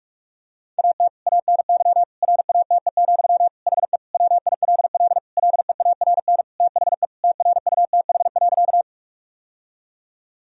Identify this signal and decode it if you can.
Morse code, heard as a keyed tone, about 46 wpm, 700 Hz. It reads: AT ANY RATE, HE WILL LEARN THE TRUTH.
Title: Sentences from Top 1000 Words - Review Part 4 - 45wpm